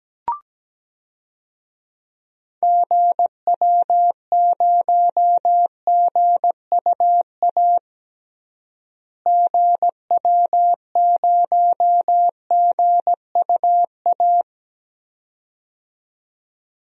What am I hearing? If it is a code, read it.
Morse code, heard as a keyed tone, about 17 wpm, 700 Hz. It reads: GW0GUA GW0GUA
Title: DXCC-100 Call Signs - 6 Characters - Repeated 2x - 17wpm